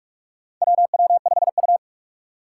Morse code, heard as a keyed tone, about 46 wpm, 700 Hz. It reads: WW5U
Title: USA Call Signs - 4 Characters - 2x Character Spacing - 45wpm